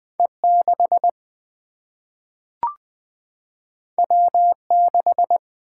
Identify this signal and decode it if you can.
Morse code, heard as a keyed tone, about 20 wpm, 700 Hz. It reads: E 6 W 6
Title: Single-Letter Single-Number 20wpm